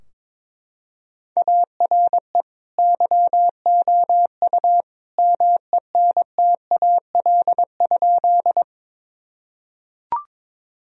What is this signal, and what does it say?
Morse code, heard as a keyed tone, about 22 wpm, 700 Hz. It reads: ARE YOU MENTAL?
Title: Sentences from Top 1000 Words - Review Part 4 - 22wpm